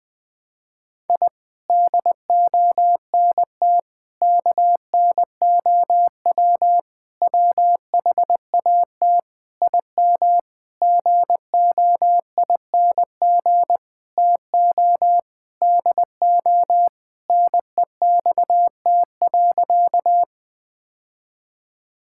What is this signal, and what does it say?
Morse code, heard as a keyed tone, about 21 wpm, 700 Hz. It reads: I DONT KNOW WHAT IM GOING TO DO NEXT.
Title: Sentences with 8-Word Phrases - 20wpm